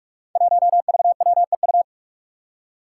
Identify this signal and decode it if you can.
Morse code, heard as a keyed tone, about 43 wpm, 700 Hz. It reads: 1VWEU
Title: Five Challenging Letter-Number Combinations 45wpm